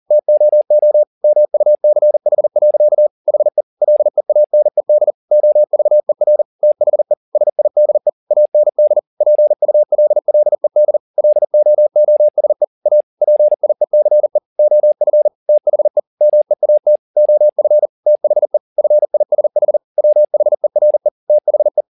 Morse code, heard as a keyed tone, about 40 wpm, 600 Hz. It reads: TOO MUCH. HE LEANED OVER THE SIDE AND PULLED LOOSE A PIECE OF THE MEAT OF THE FISH WHERE THE
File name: the old man and the sea 2 40wpm